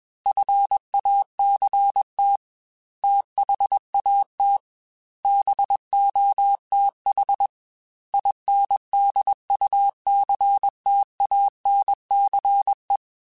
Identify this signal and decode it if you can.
Morse code, heard as a keyed tone, about 21 wpm, 800 Hz. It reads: FACT THAT BOTH INDUCTANCE
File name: Technical Subjects / Technical Subjects-mp3s / Reactance @20